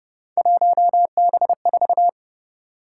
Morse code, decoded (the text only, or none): 1B4